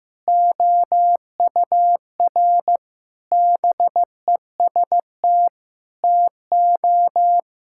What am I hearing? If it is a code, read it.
Morse code, heard as a keyed tone, about 15 wpm, 700 Hz. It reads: OUR BEST TO
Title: Sentences from Top 1000 Words - Review Part 5 - 15wpm